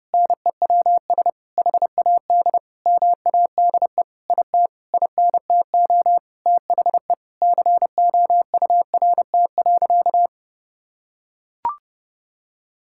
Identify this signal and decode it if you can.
Morse code, heard as a keyed tone, about 30 wpm, 700 Hz. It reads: NEWS HAD MADE IT INTO THE COURT.
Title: Sentences from Top 1600 Words 30wpm